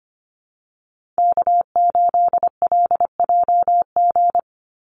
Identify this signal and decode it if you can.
Morse code, heard as a keyed tone, about 25 wpm, 700 Hz. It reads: K8LJG
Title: NA SOTA Chaser Callsigns 25wpm